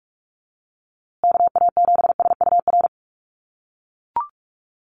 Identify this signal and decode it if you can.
Morse code, heard as a keyed tone, about 44 wpm, 700 Hz. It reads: KA7SUR
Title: USA Call Signs Repeated 2x 45wpm